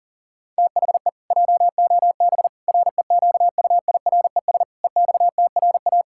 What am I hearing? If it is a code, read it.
Morse code, heard as a keyed tone, about 40 wpm, 700 Hz. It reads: THE JOB REQUIRES EXTRA
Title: Pangram Challenge v2 - 40wpm